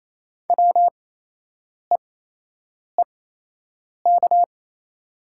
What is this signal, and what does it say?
Morse code, heard as a keyed tone, about 28 wpm, spaced as for 8 wpm, 700 Hz. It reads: WEEK
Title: 8x Character Spacing - Top 500 Words - 28wpm